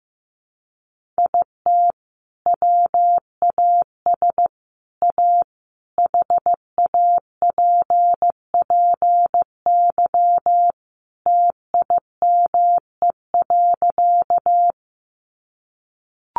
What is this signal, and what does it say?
Morse code, heard as a keyed tone, about 15 wpm, 700 Hz. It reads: IT WAS A HAPPY TIME.
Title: Sentences from Top 1000 Words 15wpm